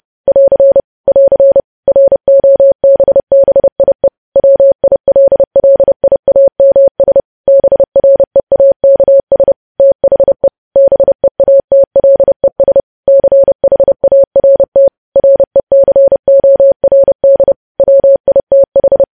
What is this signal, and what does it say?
Morse code, heard as a keyed tone, about 30 wpm, 550 Hz. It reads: <AR> <AR> ROBBIE WILLIAMS BREAKS THE BEATLES CHART RECORD WITH